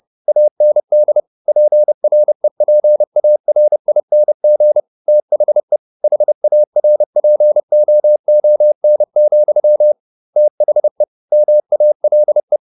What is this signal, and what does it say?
Morse code, heard as a keyed tone, about 30 wpm, 600 Hz. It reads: AND PREPARING THE HARPOON, THE MALE
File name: the old man and the sea / the old man and the sea 1 30wpm